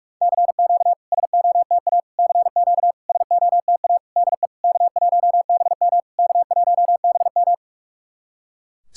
Morse code, heard as a keyed tone, about 41 wpm, 690 Hz. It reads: CQ SOTA CQ SOTA DE K1BM K1BM
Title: SOTA QSO Practice 45wpm